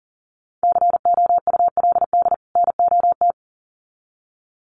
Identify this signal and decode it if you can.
Morse code, heard as a keyed tone, about 38 wpm, 700 Hz. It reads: COULD NOT
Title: Sets of 2 Words 40wpm